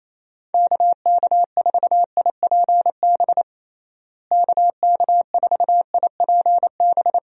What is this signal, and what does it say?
Morse code, heard as a keyed tone, about 28 wpm, 700 Hz. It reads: KK4IPB KK4IPB
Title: USA Call Signs - 6 Characters - Repeated 2x - 28wpm